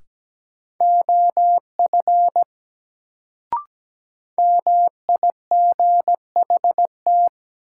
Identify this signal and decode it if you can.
Morse code, heard as a keyed tone, about 17 wpm, 700 Hz. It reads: OF MIGHT
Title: Sets of 2 Words - New Word Drill - 17wpm